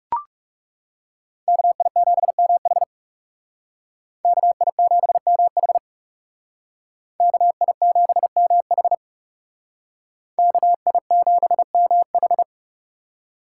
Morse code, heard as a keyed tone, about 39 wpm, 700 Hz. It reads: KI7MH KI7MH KI7MH KI7MH